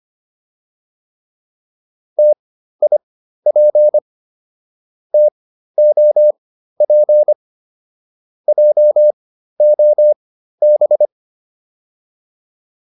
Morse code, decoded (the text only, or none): TIP TOP JOB